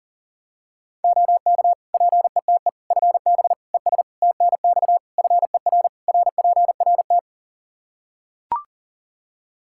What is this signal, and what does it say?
Morse code, heard as a keyed tone, about 40 wpm, 700 Hz. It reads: OK PETE FB ES TNX FER RPRT
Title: Practice QSO - Exchange 2 - Info - 40wpm